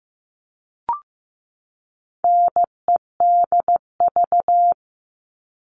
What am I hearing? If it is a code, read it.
Morse code, heard as a keyed tone, about 15 wpm, 700 Hz. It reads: NEDV
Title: All Cut Numbers - 4-Digit 15wpm